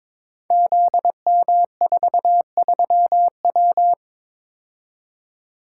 Morse code, heard as a keyed tone, about 22 wpm, 700 Hz. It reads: ZM43W
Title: Code Groups - 5 Letter:Numbers - 22wpm